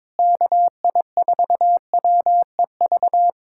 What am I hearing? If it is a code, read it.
Morse code, heard as a keyed tone, about 22 wpm, 700 Hz. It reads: KI4WEV
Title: USA Callsign DE Callsign - 6 Characters - 22wpm